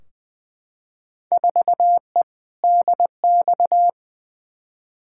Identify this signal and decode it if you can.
Morse code, heard as a keyed tone, about 20 wpm, 700 Hz. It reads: 4E DX
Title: Field Day Exchanges Encore - 20wpm